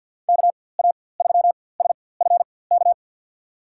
Morse code, heard as a keyed tone, about 50 wpm, spaced as for 23 wpm, 700 Hz. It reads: KA3SFK
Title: USA Call Signs - 6 Characters - 4x Character Spacing - 50wpm